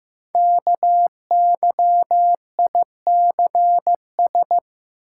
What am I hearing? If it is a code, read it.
Morse code, heard as a keyed tone, about 15 wpm, 700 Hz. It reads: KYICS
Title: Code Groups - 5 Letters - 15wpm